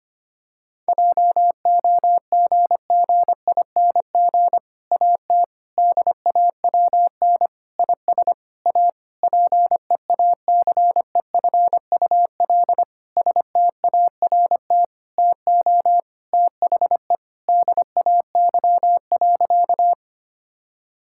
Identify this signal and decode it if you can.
Morse code, heard as a keyed tone, about 25 wpm, 700 Hz. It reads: JOGGING AT DAWN IS A PEACEFUL START TO THE DAY.
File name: practice / 25wpm / Difficult Sentences with Hints - 25wpm